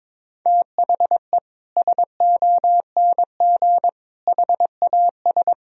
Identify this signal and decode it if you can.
Morse code, heard as a keyed tone, about 22 wpm, 700 Hz. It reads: THE SONG HAS
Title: Sentences from Top 1000 Words - Review Part 3 - 22wpm